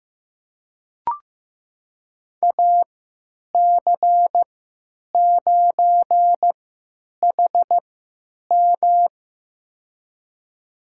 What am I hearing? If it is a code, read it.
Morse code, heard as a keyed tone, about 15 wpm, spaced as for 9 wpm, 700 Hz. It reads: AC9HM
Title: USA Call Signs - 5 Characters - 3x Character Spacing - 15wpm